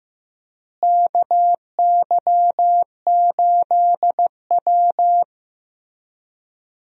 Morse code, heard as a keyed tone, about 15 wpm, 700 Hz. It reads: KY8W
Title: USA Call Signs - 4 Characters - Repeated 3x - 15wpm